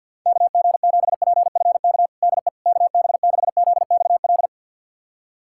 Morse code, heard as a keyed tone, about 52 wpm, 700 Hz. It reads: KG7PFK DE KB6ZXL
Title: USA Callsign DE Callsign - 6 Characters - 50wpm